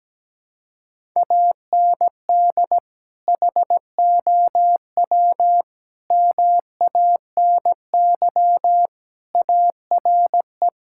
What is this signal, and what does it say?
Morse code, heard as a keyed tone, about 17 wpm, 700 Hz. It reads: AND HOW MANY ARE
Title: Sentences from Top 1000 Words - Review Part 6 - 17wpm